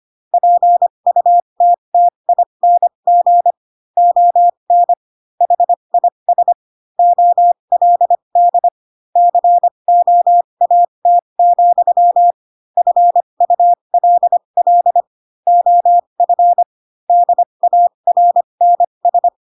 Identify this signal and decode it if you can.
Morse code, heard as a keyed tone, about 25 wpm, spaced as for 22 wpm, 700 Hz. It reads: PUTTING ON HIS OLD COAT, FULL OF DARNS